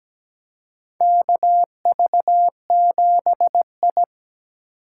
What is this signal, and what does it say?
Morse code, heard as a keyed tone, about 17 wpm, 700 Hz. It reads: KV7I
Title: USA Call Signs - 4 Characters - 6x Character Spacing - 17wpm